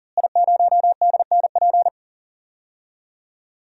Morse code, heard as a keyed tone, about 39 wpm, 700 Hz. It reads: I0DNP
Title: DXCC-100 Call Signs - 5 Characters - Repeated 3x - 40wpm